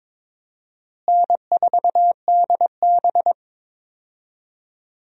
Morse code, heard as a keyed tone, about 22 wpm, 700 Hz. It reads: N4DB
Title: USA Call Signs - 4 Characters - Repeated 2x - 22wpm